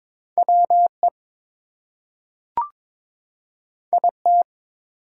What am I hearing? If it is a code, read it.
Morse code, heard as a keyed tone, about 22 wpm, 700 Hz. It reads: WE IT
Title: Two Letter Words - 22wpm